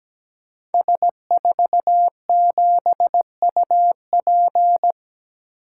Morse code, heard as a keyed tone, about 17 wpm, 700 Hz. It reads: S47UP
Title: Code Groups - 5 Letter:Numbers - 17wpm